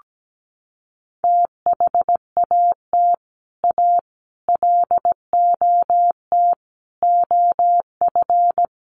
Morse code, heard as a keyed tone, about 17 wpm, 700 Hz. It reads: THAT A LOT OF
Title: Sets of 5 Words 17wpm